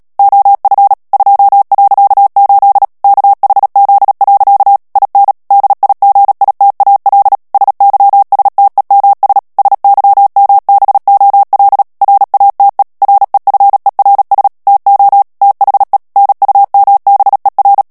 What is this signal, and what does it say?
Morse code, heard as a keyed tone, about 37 wpm, 800 Hz. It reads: OF 2.8 KHZ. IN DIGITAL SYSTEMS SYMBOL RATE REFERS TO THE NUMBER
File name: morse mp3 / ANGLAIS texte clair 3 vitesse 35 mots minute